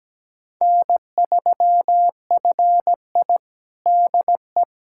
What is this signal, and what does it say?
Morse code, heard as a keyed tone, about 17 wpm, 700 Hz. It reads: N3FI DE
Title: USA Callsign DE Callsign 17wpm